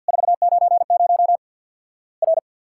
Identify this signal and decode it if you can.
Morse code, heard as a keyed tone, about 50 wpm, 700 Hz. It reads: 490 R
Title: Worked All Europe DX Contest - QTC Exchange with Repeats - 50wpm